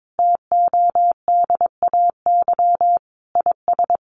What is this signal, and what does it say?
Morse code, heard as a keyed tone, about 22 wpm, 700 Hz. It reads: TODAY IS
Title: Sentences from Top 1400 Words 22wpm